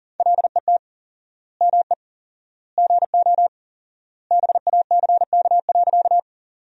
Morse code, heard as a keyed tone, about 40 wpm, 700 Hz. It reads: LET ME GO BACK.
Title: 4x Word-Spacing - Sentences from Top 300 Words - 40wpm